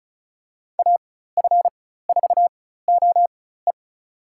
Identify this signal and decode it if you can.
Morse code, heard as a keyed tone, about 34 wpm, spaced as for 16 wpm, 700 Hz. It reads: AF4OE
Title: USA Call Signs - 4x Character Spacing - 35wpm